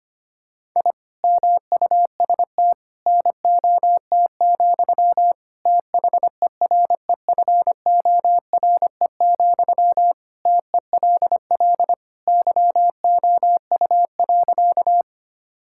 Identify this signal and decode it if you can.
Morse code, heard as a keyed tone, about 25 wpm, 700 Hz. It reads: I MUST NOT, THEREFORE, TELL YOU.
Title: Sentences from Top 1000 Words - Review Part 5 - 25wpm